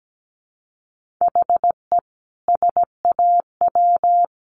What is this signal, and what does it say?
Morse code, heard as a keyed tone, about 17 wpm, 700 Hz. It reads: HE SAW
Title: Sentences from Top 700 Words 17wpm